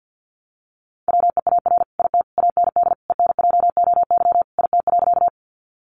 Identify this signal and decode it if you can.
Morse code, heard as a keyed tone, about 50 wpm, 700 Hz. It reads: WEAR IT AND ENJOY IT.
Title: Sentences from Top 1500 Words 50wpm